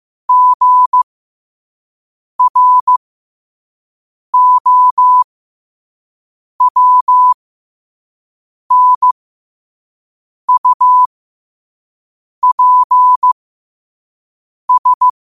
Morse code, heard as a keyed tone, about 15 wpm, spaced as for 5 wpm, 1000 Hz. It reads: GROWNUPS
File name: qotdfiles / 5wpm